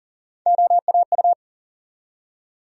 Morse code, heard as a keyed tone, about 40 wpm, 700 Hz. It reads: OAU
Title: USA Call Sign Postfixes - 3 Letters 40wpm